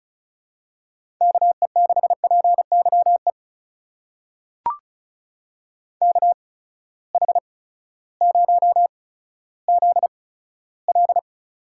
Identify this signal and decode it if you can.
Morse code, heard as a keyed tone, about 36 wpm, spaced as for 16 wpm, 700 Hz. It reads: KE6PYE KH0ZL
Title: USA Call Signs - 8x Character Spacing - 35wpm